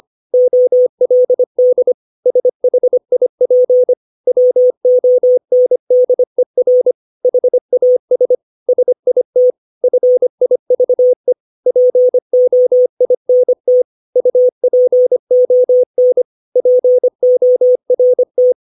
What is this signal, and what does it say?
Morse code, heard as a keyed tone, about 25 wpm, 500 Hz. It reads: OLD SHIP WONDER HAS SIT FIVE POINT UPON PORT
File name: 632 words / 632 Words 25WPM